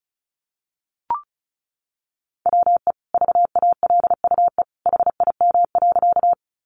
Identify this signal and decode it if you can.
Morse code, heard as a keyed tone, about 34 wpm, 700 Hz. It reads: WE VALUE HIM.
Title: Sentences from Top 600 Words 35wpm